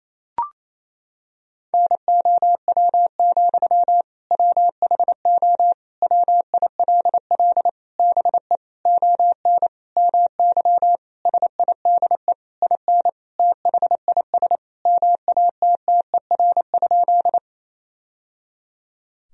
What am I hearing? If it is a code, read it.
Morse code, heard as a keyed tone, about 28 wpm, 700 Hz. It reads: NOW, WHO WILL BE ON MY SIDE IN THIS MATTER?
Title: Sentences from Top 1000 Words - Review Part 1 - 28wpm